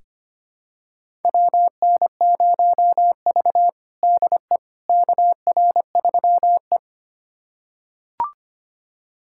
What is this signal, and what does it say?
Morse code, heard as a keyed tone, about 25 wpm, 700 Hz. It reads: WN0V DE KR3E
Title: USA Callsign DE Callsign - 4 Characters - 25wpm